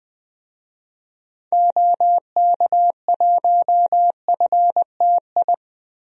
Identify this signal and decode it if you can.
Morse code, heard as a keyed tone, about 20 wpm, 700 Hz. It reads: OK1FTI
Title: DXCC-100 Call Signs - 6 Characters - Repeated 2x - 20wpm